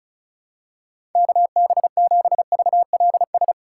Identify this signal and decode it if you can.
Morse code, heard as a keyed tone, about 34 wpm, 700 Hz. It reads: KB7VLS
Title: USA Call Signs - 6 Characters - 2x Character Spacing - 35wpm